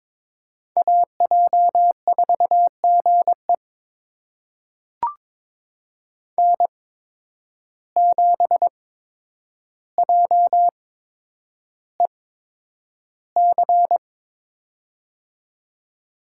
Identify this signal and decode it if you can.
Morse code, heard as a keyed tone, about 22 wpm, spaced as for 9 wpm, 700 Hz. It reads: AJ4GE N7JEC
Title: USA Call Signs - 5 Characters - 8x Character Spacing - 22wpm